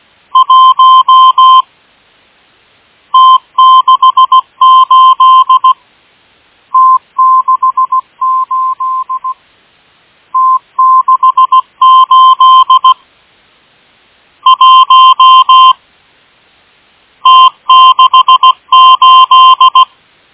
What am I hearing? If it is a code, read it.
Morse code, heard as a keyed tone, about 16 wpm, 1040 Hz. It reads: 1 T68 T68 T68 1 T68
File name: M12- -1830utc-8157khz